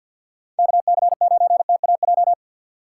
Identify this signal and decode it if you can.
Morse code, heard as a keyed tone, about 52 wpm, 700 Hz. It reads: KC9TAJ
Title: USA Call Signs - 6 Characters - 4x Character Spacing - 50wpm